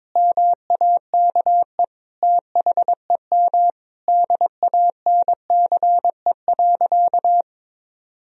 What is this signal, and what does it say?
Morse code, heard as a keyed tone, about 22 wpm, 700 Hz. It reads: MAKE THEM DANCE.
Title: Sentences from Top 1000 Words - Review Part 4 - 22wpm